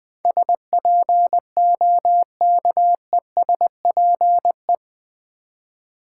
Morse code, heard as a keyed tone, about 21 wpm, 700 Hz. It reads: SPOKESPE
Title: Build A Word - 4 Letters At A Time - 12-Letter Words - 20wpm